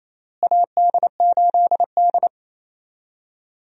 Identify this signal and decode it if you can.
Morse code, heard as a keyed tone, about 28 wpm, 700 Hz. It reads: AD8D
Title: USA Call Signs - 4 Characters - Repeated 2x - 28wpm